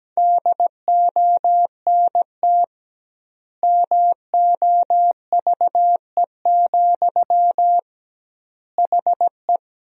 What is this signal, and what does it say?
Morse code, heard as a keyed tone, about 17 wpm, 700 Hz. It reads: DONT MOVE, HE
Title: 2x Word-Spacing - Sentences from Top 300 Words - 17wpm